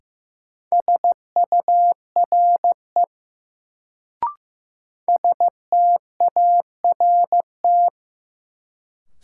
Morse code, heard as a keyed tone, about 15 wpm, 700 Hz. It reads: SURE START
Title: Sentences from Top 300 Words - New Word Drill - 15wpm